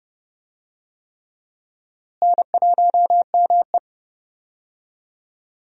Morse code, heard as a keyed tone, about 31 wpm, 700 Hz. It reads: N1ME
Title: USA Call Signs - 4 Characters - Repeated 3x - 30wpm